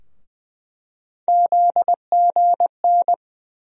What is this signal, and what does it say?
Morse code, heard as a keyed tone, about 19 wpm, 700 Hz. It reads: ZGN
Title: USA Call Sign Postfixes - 3 Letters 20wpm